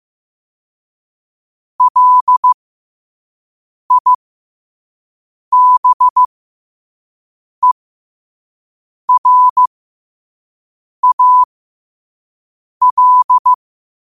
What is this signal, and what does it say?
Morse code, heard as a keyed tone, about 15 wpm, spaced as for 5 wpm, 1000 Hz. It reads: LIBERAL
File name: qotdfiles / 5wpm